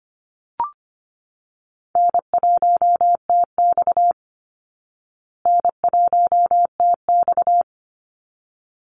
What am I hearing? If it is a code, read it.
Morse code, heard as a keyed tone, about 25 wpm, 700 Hz. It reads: N1TX N1TX
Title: Call Signs Repeated 3x - 25wpm